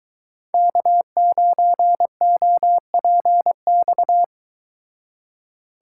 Morse code, heard as a keyed tone, about 21 wpm, 700 Hz. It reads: K9OPX